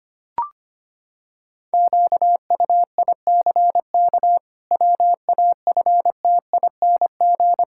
Morse code, heard as a keyed tone, about 25 wpm, 700 Hz. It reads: QUICK WAFTING
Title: Pangram Challenge v2 - 25wpm